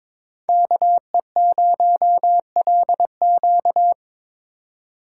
Morse code, heard as a keyed tone, about 21 wpm, 700 Hz. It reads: KE0LQ
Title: USA Call Signs - 5 Characters - 2x Character Spacing - 22wpm